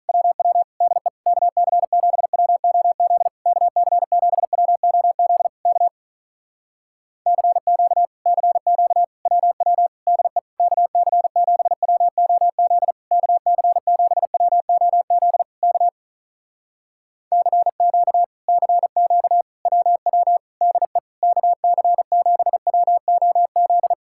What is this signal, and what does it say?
Morse code, heard as a keyed tone, about 43 wpm, 700 Hz. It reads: WW DE KC7WOZ KC7WOZ K CQ CQ WW DE KC7WOZ KC7WOZ K CQ CQ WW DE KC7WOZ
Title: Speed-Racing Common QSO Phrases - 35wpm